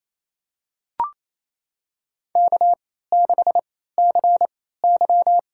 Morse code, heard as a keyed tone, about 28 wpm, spaced as for 16 wpm, 700 Hz. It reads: K6CY
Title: USA Call Signs - 4 Characters - 3x Character Spacing - 28wpm